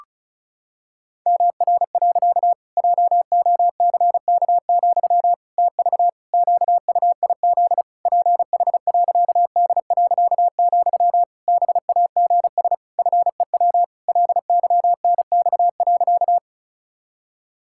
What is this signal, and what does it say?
Morse code, heard as a keyed tone, about 36 wpm, 700 Hz. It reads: MR. JOCK, TV QUIZ PH.D., BAGS FEW LYNX.
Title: Pangram Challenge v2 - 35wpm